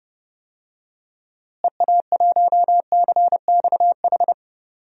Morse code, heard as a keyed tone, about 30 wpm, 700 Hz. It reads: EA1CXH
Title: DXCC-100 Call Signs - 6 Characters - Repeated 3x - 30wpm